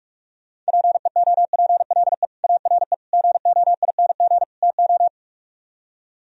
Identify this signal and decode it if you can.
Morse code, heard as a keyed tone, about 65 wpm, spaced as for 54 wpm, 700 Hz. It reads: PEOPLE ARE GOING TO
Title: Speed-Racing - Sets of 4 Words 45wpm